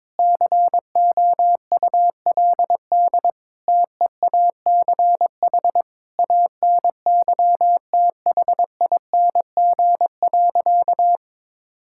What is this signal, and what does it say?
Morse code, heard as a keyed tone, about 22 wpm, 700 Hz. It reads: COULD TEACH ANYTHING.
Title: Sentences from Top 1000 Words - Review Part 6 - 22wpm